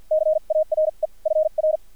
Morse code, heard as a keyed tone, about 50 wpm, spaced as for 41 wpm, 620 Hz, with radio noise. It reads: CARE FR
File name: archived / CRSnet